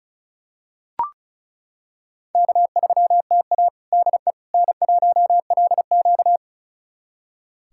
Morse code, heard as a keyed tone, about 34 wpm, 700 Hz. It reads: K3TA DE N1LQ
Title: USA Callsign DE Callsign - 4 Characters - 35wpm